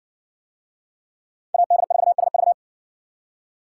Morse code, heard as a keyed tone, about 60 wpm, 700 Hz. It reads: AD4SV